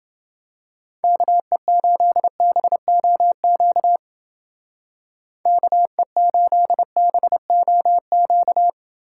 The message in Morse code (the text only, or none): KE8BOQ KE8BOQ